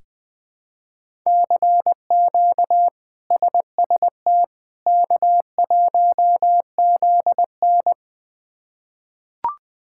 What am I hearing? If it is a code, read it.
Morse code, heard as a keyed tone, about 20 wpm, 700 Hz. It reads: CQ SST K1ZN
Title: K1USN SST QSO Practice - 20wpm